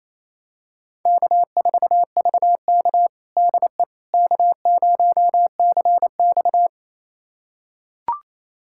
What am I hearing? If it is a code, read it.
Morse code, heard as a keyed tone, about 28 wpm, 700 Hz. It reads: K4VK DE K0CX
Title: USA Callsign DE Callsign - 4 Characters - 28wpm